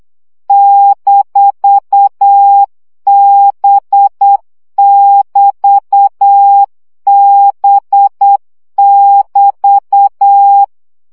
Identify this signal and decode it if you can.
Morse code, heard as a keyed tone, about 8 wpm, 800 Hz. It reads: -B<BT>B<BT>
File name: CW gyakorlas b 6 egyenlo minusz 7 baud